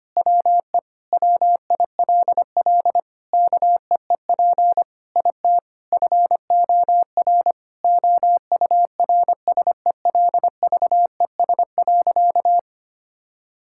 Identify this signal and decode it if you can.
Morse code, heard as a keyed tone, about 25 wpm, 700 Hz. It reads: WE WILL KEEP IT FOR OURSELVES.
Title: Sentences from Top 1600 Words 25wpm